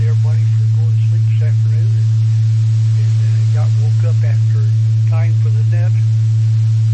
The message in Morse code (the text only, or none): none